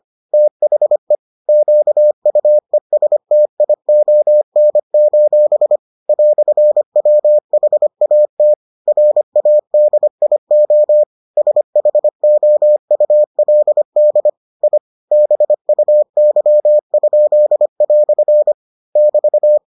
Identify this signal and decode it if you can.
Morse code, heard as a keyed tone, about 25 wpm, 600 Hz. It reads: THE QUESTION: "WHAT RADIO SHOULD I BUY?" <BT>